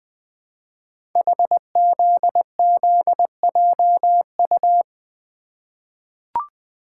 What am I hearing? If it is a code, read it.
Morse code, heard as a keyed tone, about 20 wpm, 700 Hz. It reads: HZZJU
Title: Code Groups - 5 Letters - 20wpm